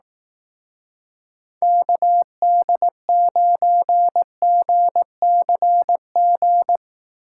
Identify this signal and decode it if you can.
Morse code, heard as a keyed tone, about 18 wpm, 700 Hz. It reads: KD9GCG
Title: Speed Racing - USA 2x3 Call Signs 15wpm